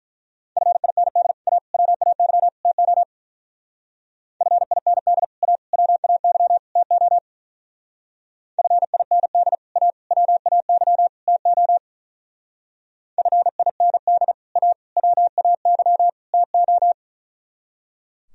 Morse code, heard as a keyed tone, about 42 wpm, 700 Hz. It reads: FIND A WAY TO FIND A WAY TO FIND A WAY TO FIND A WAY TO
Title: Speed-Racing - Sets of 4 Words 35wpm